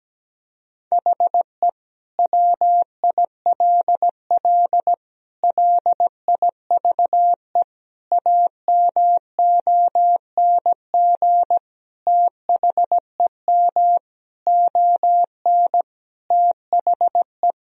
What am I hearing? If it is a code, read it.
Morse code, heard as a keyed tone, about 17 wpm, 700 Hz. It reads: HE WILL LIVE AMONG THEM ON THE
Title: Sentences from Top 1000 Words - Review Part 3 - 17wpm